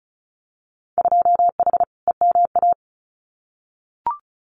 Morse code, heard as a keyed tone, about 36 wpm, 690 Hz. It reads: 2H EMA
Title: Winter Field Day Exchanges 35wpm